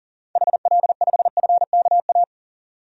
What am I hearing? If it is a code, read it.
Morse code, heard as a keyed tone, about 40 wpm, 700 Hz. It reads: HL5FKA
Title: DXCC-100 Call Signs - 6 Characters - 40wpm